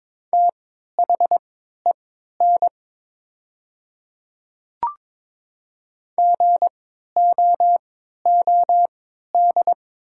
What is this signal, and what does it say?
Morse code, heard as a keyed tone, about 22 wpm, spaced as for 12 wpm, 700 Hz. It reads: THEN GOOD
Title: 3x Character Spacing - Top 200 Words - 22wpm